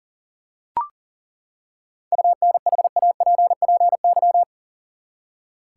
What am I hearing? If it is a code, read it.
Morse code, heard as a keyed tone, about 38 wpm, 700 Hz. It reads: UNHAPPY
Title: Seven Letter Words - 40wpm